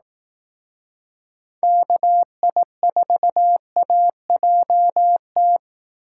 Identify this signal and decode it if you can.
Morse code, heard as a keyed tone, about 18 wpm, 700 Hz. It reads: KI4AJT
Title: Speed Racing - USA Call Signs 15wpm